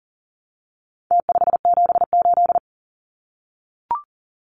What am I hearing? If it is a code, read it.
Morse code, heard as a keyed tone, about 40 wpm, 700 Hz. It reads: T578
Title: Numbers - Contest Style - 4-Digit 40wpm